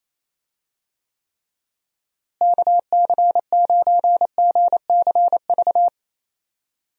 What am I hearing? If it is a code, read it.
Morse code, heard as a keyed tone, about 27 wpm, 700 Hz. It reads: KC9GCV
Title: USA Call Signs - 6 Characters - Repeated 2x - 28wpm